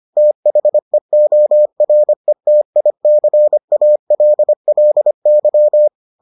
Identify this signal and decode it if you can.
Morse code, heard as a keyed tone, about 25 wpm, 600 Hz. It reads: THEORETICALLY